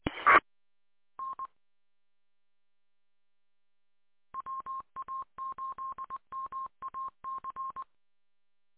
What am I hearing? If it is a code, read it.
Morse code, heard as a keyed tone, about 20 wpm, 1060 Hz. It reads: T N WA8MAC